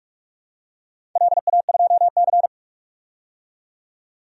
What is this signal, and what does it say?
Morse code, heard as a keyed tone, about 46 wpm, 700 Hz. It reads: LA2C
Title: DXCC-100 Call Signs - 4 Characters - 45wpm